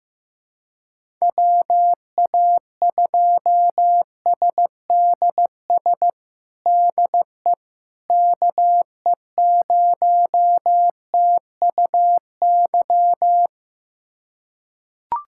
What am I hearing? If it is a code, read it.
Morse code, heard as a keyed tone, about 15 wpm, 700 Hz. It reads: WA2SDS DE KE0TUY
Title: USA Callsign DE Callsign - 6 Characters - 15wpm